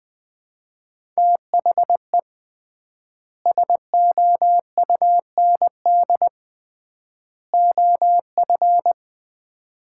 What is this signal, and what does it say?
Morse code, heard as a keyed tone, about 20 wpm, 700 Hz. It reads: THE SOUND OF
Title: 3x Word-Spacing - Sets of 3 Words - 20wpm